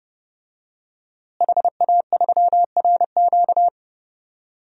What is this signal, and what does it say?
Morse code, heard as a keyed tone, about 32 wpm, 700 Hz. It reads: HA3RQ
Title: DXCC-100 Call Signs - 5 Characters - Repeated 2x - 30wpm